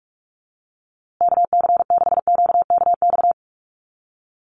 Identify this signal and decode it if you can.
Morse code, heard as a keyed tone, about 44 wpm, 690 Hz. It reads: KC6QKX
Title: USA Call Signs Repeated 3x 45wpm